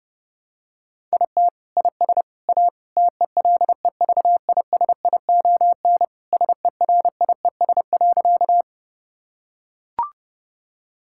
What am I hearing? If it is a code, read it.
Morse code, heard as a keyed tone, about 32 wpm, 700 Hz. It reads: IT IS A TELEVISION SERIES.
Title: Sentences from Top 1000 Words - Review Part 5 - 30wpm